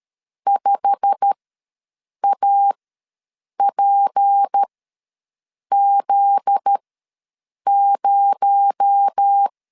Morse code, heard as a keyed tone, about 13 wpm, spaced as for 7 wpm, 800 Hz. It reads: 5APZ0
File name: LEZIONI / 10 SIGNS / 10A SLASH ws 12wpm 3443